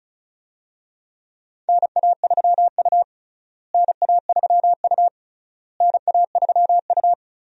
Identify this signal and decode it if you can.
Morse code, heard as a keyed tone, about 36 wpm, 700 Hz. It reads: NA3U NA3U NA3U
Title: USA Call Signs - 4 Characters - Repeated 3x - 35wpm